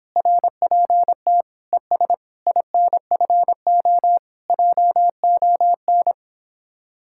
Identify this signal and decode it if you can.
Morse code, heard as a keyed tone, about 31 wpm, 700 Hz. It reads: RPT ES INFO JON